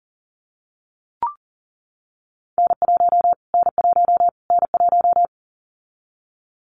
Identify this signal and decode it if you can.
Morse code, heard as a keyed tone, about 40 wpm, 700 Hz. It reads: N1 N1 N1
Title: Callsign Prefixes for Top 25 DXCC - Repeated 3x - 40wpm